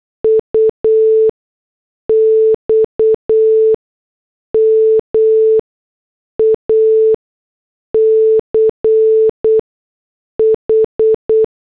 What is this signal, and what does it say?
Morse code, heard as a keyed tone, about 8 wpm, 440 Hz. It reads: U X M A C H
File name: morse / headlines